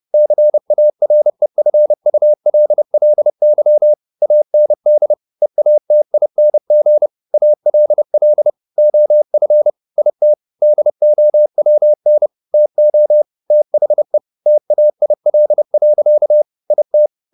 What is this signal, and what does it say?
Morse code, heard as a keyed tone, about 30 wpm, 600 Hz. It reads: CAREFULLY AND EATING ALL OF IT DOWN TO THE TAIL. IT